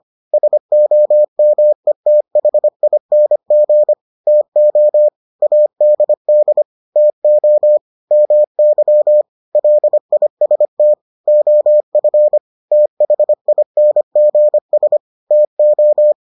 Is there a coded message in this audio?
Yes: Morse code from 0.3 to 16.1 s, heard as a keyed tone, about 25 wpm, 600 Hz. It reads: SOMETHING TO ADD TO MY LIST OF THINGS TO